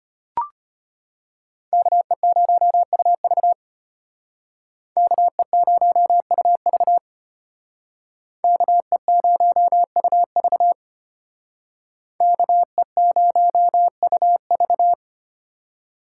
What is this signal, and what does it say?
Morse code, heard as a keyed tone, about 33 wpm, 700 Hz. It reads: KE0UV KE0UV KE0UV KE0UV